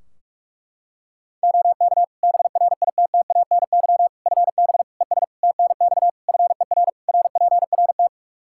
Morse code, heard as a keyed tone, about 42 wpm, 700 Hz. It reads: OK BRITTANY FB ES TNX FER RPRT
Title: K1USN SST QSO Practice 45wpm